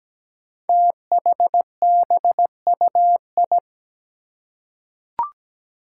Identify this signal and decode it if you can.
Morse code, heard as a keyed tone, about 17 wpm, 700 Hz. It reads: THBUI
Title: Five Challenging Letter-Number Combinations 17wpm